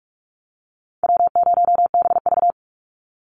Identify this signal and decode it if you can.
Morse code, heard as a keyed tone, about 44 wpm, 700 Hz. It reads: W0BV
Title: Easy Call Signs - 45wpm